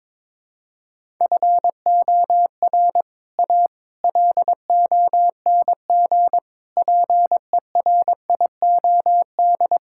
Morse code, heard as a keyed tone, about 22 wpm, 700 Hz. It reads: FOR A LONG PERIOD
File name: practice / 22wpm / Sets of 6 Words - 22wpm